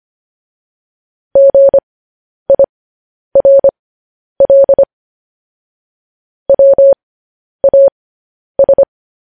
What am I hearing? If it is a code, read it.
Morse code, heard as a keyed tone, about 25 wpm, spaced as for 10 wpm, 550 Hz. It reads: GIRL WAS
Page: News Headlines in Morse Code at 10 WPM